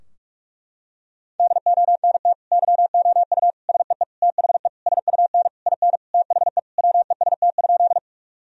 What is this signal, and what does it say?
Morse code, heard as a keyed tone, about 46 wpm, 700 Hz. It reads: DONT YOU SEE THE SUN IN THE WEST?
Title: Sentences from Top 1000 Words - Review Part 2 - 45wpm